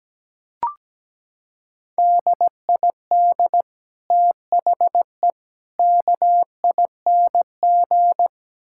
Morse code, heard as a keyed tone, about 17 wpm, 700 Hz. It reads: DID THE KING
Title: Sentences from Top 1000 Words - Review Part 6 - 17wpm